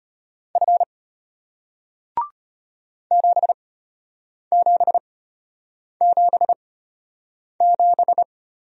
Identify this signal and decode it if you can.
Morse code, heard as a keyed tone, about 33 wpm, spaced as for 8 wpm, 700 Hz. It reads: F 7777